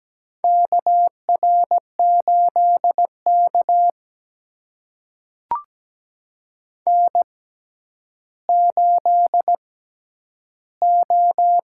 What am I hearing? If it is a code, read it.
Morse code, heard as a keyed tone, about 17 wpm, spaced as for 10 wpm, 700 Hz. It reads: KR8K N8O
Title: USA Call Signs - 6x Character Spacing - 17wpm